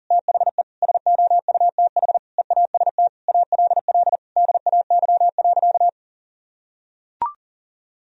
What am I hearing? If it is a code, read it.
Morse code, heard as a keyed tone, about 40 wpm, 700 Hz. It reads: THE SOUTH EAST ALL DAY.
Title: Sentences from Top 1000 Words - Review Part 3 - 40wpm